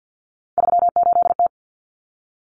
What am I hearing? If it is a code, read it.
Morse code, heard as a keyed tone, about 50 wpm, 700 Hz. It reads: 38T
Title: Common Cut Numbers - 3-Digit 50wpm